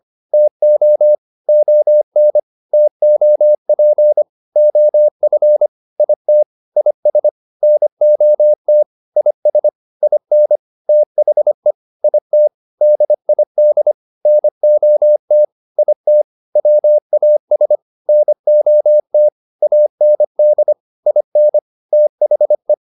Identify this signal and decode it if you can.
Morse code, heard as a keyed tone, about 25 wpm, 600 Hz. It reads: TO ON TOP OF IT IS NOT IS IN THE IT DID NOT IT WAS NOT AND IN THE